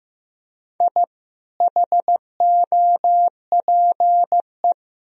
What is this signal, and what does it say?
Morse code, heard as a keyed tone, about 15 wpm, 700 Hz. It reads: I HOPE
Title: Sentences from Top 1000 Words - Review Part 2 - 15wpm